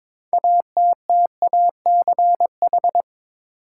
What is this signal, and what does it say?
Morse code, heard as a keyed tone, about 22 wpm, 700 Hz. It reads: ATTACH
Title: Six Letter Words - 22wpm